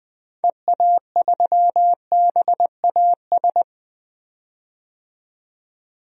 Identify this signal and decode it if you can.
Morse code, heard as a keyed tone, about 20 wpm, 700 Hz. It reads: EA3BAS
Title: DXCC-100 Call Signs - 6 Characters - 20wpm